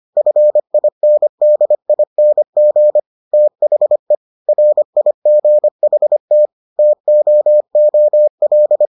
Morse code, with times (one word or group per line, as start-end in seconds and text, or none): 0.2-3.0 FINDING
3.3-4.1 THE
4.5-6.5 RIGHT
6.8-8.9 TOOL